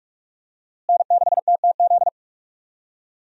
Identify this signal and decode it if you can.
Morse code, heard as a keyed tone, about 46 wpm, 700 Hz. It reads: N6TTZ
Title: USA Call Signs - 5 Characters - Repeated 3x - 45wpm